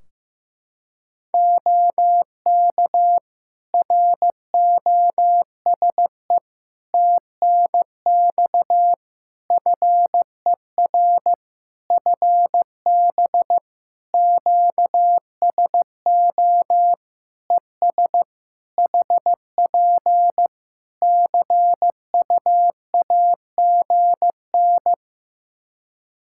Morse code, heard as a keyed tone, about 15 wpm, 700 Hz. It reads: OK ROSE TNX FER FB QSO ES HP CUAGN
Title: Practice QSO - Exchange 4 - Ending - 15wpm